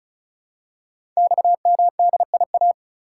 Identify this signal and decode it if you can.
Morse code, heard as a keyed tone, about 36 wpm, 700 Hz. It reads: XMDIA
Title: Code Groups - 5 Letters - 35wpm